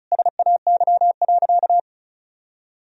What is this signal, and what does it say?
Morse code, heard as a keyed tone, about 36 wpm, 700 Hz. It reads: SAY.
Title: Sentences from Top 1000 Words - Review Part 2 - 35wpm